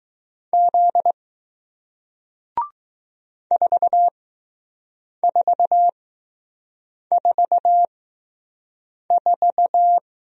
Morse code, heard as a keyed tone, about 20 wpm, spaced as for 6 wpm, 700 Hz. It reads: Z 4444